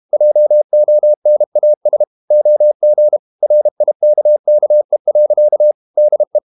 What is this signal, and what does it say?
Morse code, heard as a keyed tone, about 32 wpm, 600 Hz. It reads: JONAS OG RIKKE. DE
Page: Ferie 32 wpm